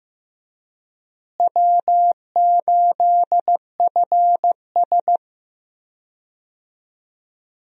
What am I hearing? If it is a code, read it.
Morse code, heard as a keyed tone, about 16 wpm, 700 Hz. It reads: W8FS
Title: USA Call Signs - 4 Characters - Repeated 3x - 15wpm